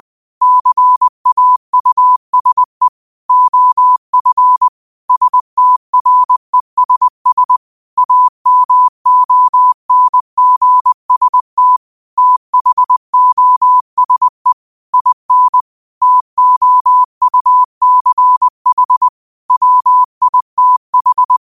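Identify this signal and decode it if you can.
Morse code, heard as a keyed tone, about 20 wpm, 1000 Hz. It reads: CAUSE OF STRESS AMONGST THOSE IN TOUCH WITH
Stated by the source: 20 WPM Morse code quotes, QOTD at 20 WPM.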